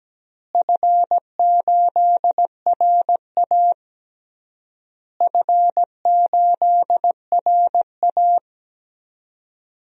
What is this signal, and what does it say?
Morse code, heard as a keyed tone, about 17 wpm, 700 Hz. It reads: F8RA F8RA
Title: DXCC-100 Call Signs - 4 Characters - Repeated 3x - 17wpm